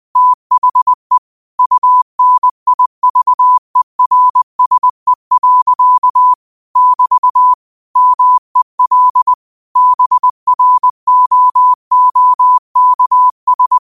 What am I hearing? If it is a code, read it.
Morse code, heard as a keyed tone, about 20 wpm, 1000 Hz. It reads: THE UNIVERSE. <BT> MEL BROOKS
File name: qotdfiles / 20wpm